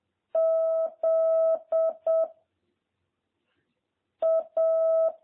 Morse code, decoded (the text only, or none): Z A